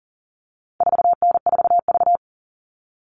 Morse code, heard as a keyed tone, about 40 wpm, 700 Hz. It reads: 4N4V
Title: All Cut Numbers - 4-Digit 40wpm